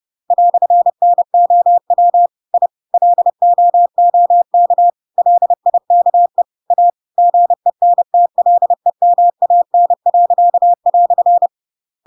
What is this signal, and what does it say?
Morse code, heard as a keyed tone, about 30 wpm, 700 Hz. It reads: "NOW I LOOK LIKE A GENTLEMAN."
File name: Pinocchio / Pinocchio ch8 30 30 ik1hge